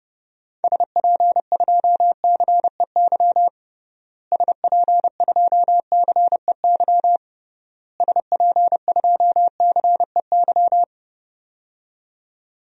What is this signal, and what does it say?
Morse code, heard as a keyed tone, about 30 wpm, 700 Hz. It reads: SP2CEY SP2CEY SP2CEY
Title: DXCC-100 Call Signs - 6 Characters - Repeated 3x - 30wpm